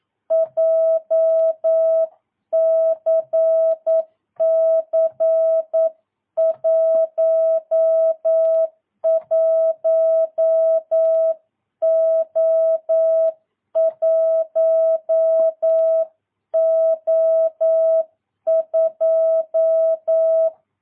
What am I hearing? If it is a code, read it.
Morse code, heard as a keyed tone, about 9 wpm, 650 Hz. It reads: JCC 11O1O2
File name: jh8cbh / 2022cw